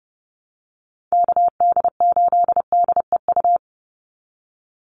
Morse code, heard as a keyed tone, about 30 wpm, 700 Hz. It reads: KD8DEU
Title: NA SOTA Chaser Callsigns 30wpm